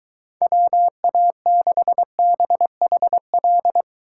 Morse code, heard as a keyed tone, about 23 wpm, 700 Hz. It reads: WA6BHL